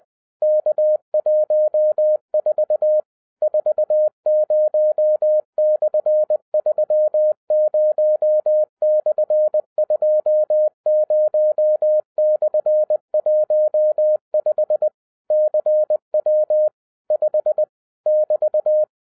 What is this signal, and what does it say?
Morse code, heard as a keyed tone, about 20 wpm, 610 Hz. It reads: K14 40/30/20/15 CW 5 <BT>
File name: 2011-07-mp3 / n23 QRP OTAKU 20wpm